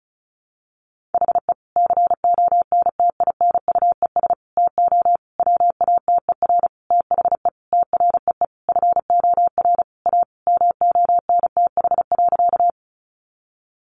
Morse code, heard as a keyed tone, about 34 wpm, 700 Hz. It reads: HE CONTINUES TO WATER THE TREE FOR A MONTH.
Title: Easy Phrases 35wpm